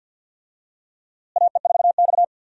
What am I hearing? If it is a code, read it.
Morse code, heard as a keyed tone, about 52 wpm, 700 Hz. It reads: AE4X
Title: USA Call Signs - 4 Characters - Repeated 3x - 50wpm